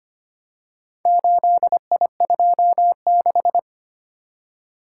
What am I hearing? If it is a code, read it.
Morse code, heard as a keyed tone, about 25 wpm, 700 Hz. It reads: 8I26
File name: practice / 25wpm / Four Letter-Number Combinations - 25wpm